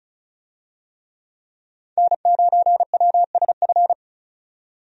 Morse code, heard as a keyed tone, about 36 wpm, 700 Hz. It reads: N9WSF
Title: USA Call Signs - 5 Characters - 35wpm